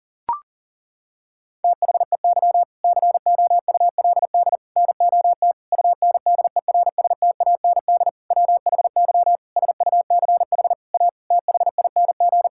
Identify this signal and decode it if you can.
Morse code, heard as a keyed tone, about 40 wpm, 700 Hz. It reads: THEY COULD NOT UNDERSTAND WHY SUCH A THING
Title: Sentences from Top 1000 Words - Review Part 6 - 40wpm